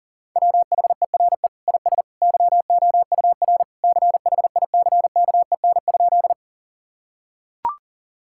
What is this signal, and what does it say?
Morse code, heard as a keyed tone, about 37 wpm, 700 Hz. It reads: WHERE IS YOUR CHICKEN?
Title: Sentences from Top 1000 Words - Review Part 5 - 40wpm